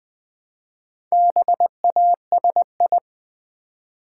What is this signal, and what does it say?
Morse code, heard as a keyed tone, about 20 wpm, 700 Hz. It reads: BASI
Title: Build A Word - 1 Letter At A Time - 5-Letter Words - 20wpm